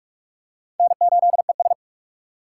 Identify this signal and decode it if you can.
Morse code, heard as a keyed tone, about 46 wpm, 700 Hz. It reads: N8ES
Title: USA Call Signs - 4 Characters - 3x Character Spacing - 45wpm